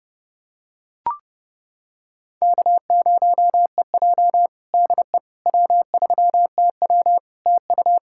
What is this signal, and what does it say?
Morse code, heard as a keyed tone, about 30 wpm, 700 Hz. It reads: K0EJ DE W3TW TU
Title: Practice QSO - Exchange 4 - Ending - 30wpm